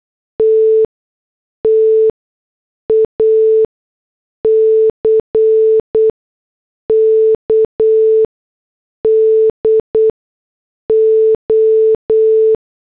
Morse code, heard as a keyed tone, about 8 wpm, spaced as for 6 wpm, 440 Hz. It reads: TTACK DO